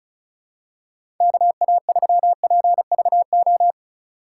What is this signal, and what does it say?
Morse code, heard as a keyed tone, about 36 wpm, 690 Hz. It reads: KA3PVO
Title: USA Call Signs - 6 Characters - 2x Character Spacing - 35wpm